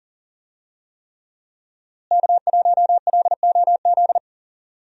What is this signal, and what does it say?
Morse code, heard as a keyed tone, about 40 wpm, 700 Hz. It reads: K1LOZ
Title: USA Call Signs - 5 Characters - Repeated 3x - 40wpm